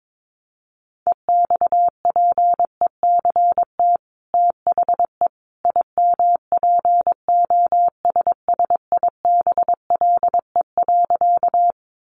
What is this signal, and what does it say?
Morse code, heard as a keyed tone, about 22 wpm, 700 Hz. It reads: EXPECT THE IMPOSSIBLE.
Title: Sentences from Top 1100 Words 22wpm